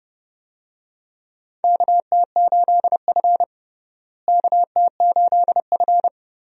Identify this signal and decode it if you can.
Morse code, heard as a keyed tone, about 32 wpm, 700 Hz. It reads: KT8F KT8F
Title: USA Call Signs - 4 Characters - Repeated 3x - 30wpm